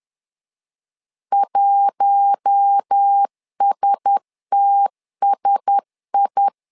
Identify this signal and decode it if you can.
Morse code, heard as a keyed tone, about 10 wpm, 800 Hz. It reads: 1STSI
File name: LEZIONI / 4 Q1S 10wpm 2133